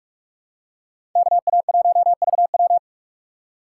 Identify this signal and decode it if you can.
Morse code, heard as a keyed tone, about 43 wpm, 700 Hz. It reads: KA1VW